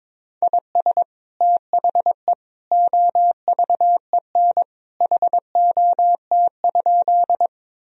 Morse code, heard as a keyed tone, about 22 wpm, 700 Hz. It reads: IS THE OVEN HOT?
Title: Sentences from Top 1000 Words - Review Part 4 - 22wpm